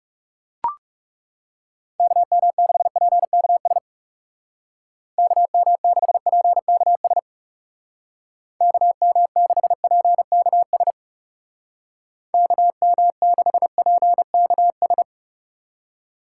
Morse code, heard as a keyed tone, about 37 wpm, 700 Hz. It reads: KM6PKS KM6PKS KM6PKS KM6PKS
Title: Speed Racing - USA Call Signs 30wpm